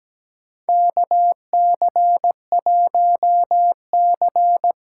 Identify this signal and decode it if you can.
Morse code, heard as a keyed tone, about 17 wpm, 700 Hz. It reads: KC1C
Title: USA Call Signs - 6 Characters - 8x Character Spacing - 17wpm